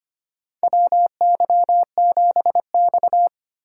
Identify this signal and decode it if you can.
Morse code, heard as a keyed tone, about 25 wpm, 700 Hz. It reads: WY7X
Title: USA Call Signs - 4 Characters - 2x Character Spacing - 25wpm